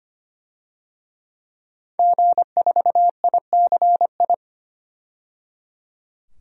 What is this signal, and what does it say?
Morse code, heard as a keyed tone, about 25 wpm, 700 Hz. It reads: G4ICI
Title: DXCC-100 Call Signs - 5 Characters - 25wpm